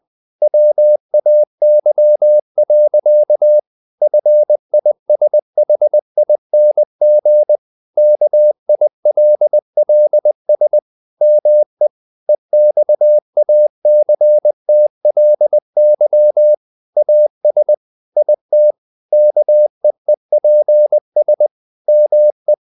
Morse code, heard as a keyed tone, about 20 wpm, 600 Hz. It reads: WAY. FISHING KILLS ME EXACTLY AS IT KEEPS ME
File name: the old man and the sea part 2 20wpm